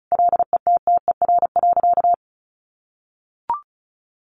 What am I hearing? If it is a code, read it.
Morse code, heard as a keyed tone, about 34 wpm, 700 Hz. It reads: LETTER.
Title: Sentences from Top 1400 Words 35wpm